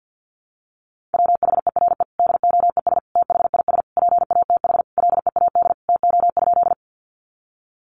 Morse code, heard as a keyed tone, about 50 wpm, 700 Hz. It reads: WHERE DOES THIS PATH LEAD TO?
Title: Sentences from Top 1400 Words 50wpm